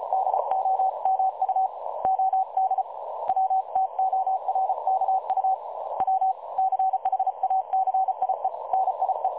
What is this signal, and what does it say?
Morse code, heard as a keyed tone, about 33 wpm, 760 Hz. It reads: DOMU OD WTORKU W CHACIE NIE
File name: dlazaawansowanych / nagraniazpasma / websdr recording 3565.0kHz